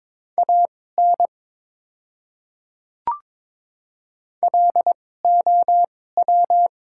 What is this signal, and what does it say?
Morse code, heard as a keyed tone, about 22 wpm, spaced as for 16 wpm, 700 Hz. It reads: AN LOW
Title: 2x Character Spacing - Top 200 Words - 22wpm